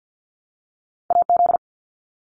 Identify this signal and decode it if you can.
Morse code, heard as a keyed tone, about 50 wpm, 700 Hz. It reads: AZ
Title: ARRL Sections 50wpm